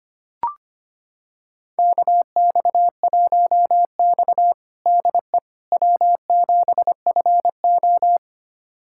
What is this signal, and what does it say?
Morse code, heard as a keyed tone, about 25 wpm, 700 Hz. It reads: KX1X DE W7FO
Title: USA Callsign DE Callsign - 4 Characters - 25wpm